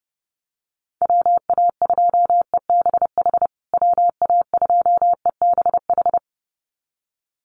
Morse code, heard as a keyed tone, about 30 wpm, 700 Hz. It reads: WA2EBH WA2EBH
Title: USA Call Signs Repeated 2x 30wpm